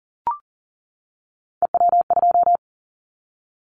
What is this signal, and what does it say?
Morse code, heard as a keyed tone, about 37 wpm, 700 Hz. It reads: EW2
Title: Callsign Prefixes for Top 50 DXCC - 40wpm